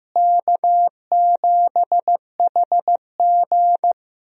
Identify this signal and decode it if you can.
Morse code, heard as a keyed tone, about 15 wpm, 700 Hz. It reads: K7HG
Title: USA Call Signs - 4 Characters - Repeated 2x - 15wpm